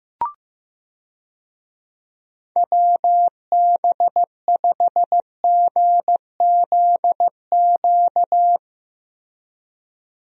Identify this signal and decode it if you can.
Morse code, heard as a keyed tone, about 15 wpm, 700 Hz. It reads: WB5GZQ
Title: USA Call Signs - 6 Characters - 15wpm